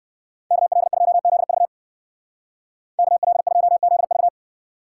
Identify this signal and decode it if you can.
Morse code, heard as a keyed tone, about 54 wpm, 700 Hz. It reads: KL2ZV KL2ZV
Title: Speed Racing - USA Call Signs 45wpm